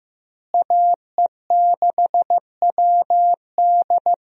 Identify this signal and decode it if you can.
Morse code, heard as a keyed tone, about 16 wpm, 700 Hz. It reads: AE6WD
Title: USA Call Signs - 5 Characters - Repeated 3x - 15wpm